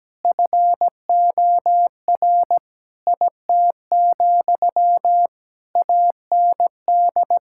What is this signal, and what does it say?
Morse code, heard as a keyed tone, about 17 wpm, 700 Hz. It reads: FOR IT, AND
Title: Sentences from Top 1000 Words - Review Part 5 - 17wpm